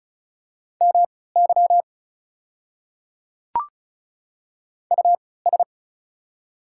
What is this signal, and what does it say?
Morse code, heard as a keyed tone, about 34 wpm, spaced as for 20 wpm, 700 Hz. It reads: MY US
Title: 3x Character Spacing - Top 200 Words - 35wpm